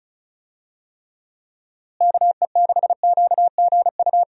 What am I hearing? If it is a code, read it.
Morse code, heard as a keyed tone, about 34 wpm, 700 Hz. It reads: KE6QGU
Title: USA Call Signs - 6 Characters - Repeated 3x - 35wpm